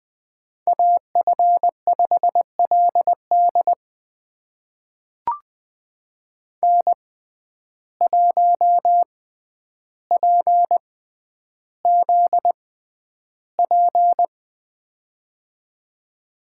Morse code, heard as a keyed tone, about 20 wpm, spaced as for 10 wpm, 700 Hz. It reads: AF5LD N1PZP
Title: USA Call Signs - 5 Characters - 6x Character Spacing - 20wpm